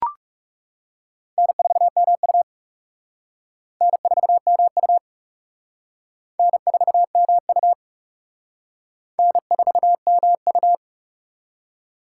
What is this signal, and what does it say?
Morse code, heard as a keyed tone, about 36 wpm, 700 Hz. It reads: N4MU N4MU N4MU N4MU
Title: Speed Racing - USA 1x2 Call Signs 30wpm